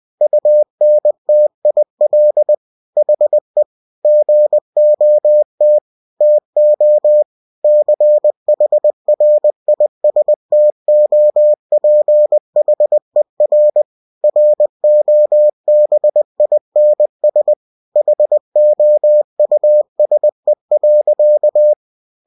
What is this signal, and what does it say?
Morse code, heard as a keyed tone, about 20 wpm, 600 Hz. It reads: UNTIL HE GOT TO CHRISTOPHER ROBINS HOUSE.